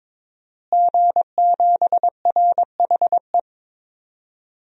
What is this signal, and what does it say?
Morse code, heard as a keyed tone, about 22 wpm, 700 Hz. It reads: G7RHE
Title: DXCC-100 Call Signs - 5 Characters - Repeated 3x - 22wpm